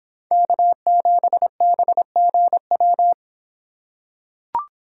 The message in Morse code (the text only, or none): K7BGW